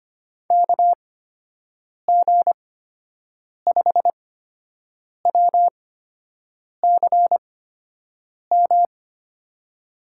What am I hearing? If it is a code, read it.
Morse code, heard as a keyed tone, about 25 wpm, spaced as for 7 wpm, 700 Hz. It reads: KG5WCM
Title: USA Call Signs - 8x Character Spacing - 25wpm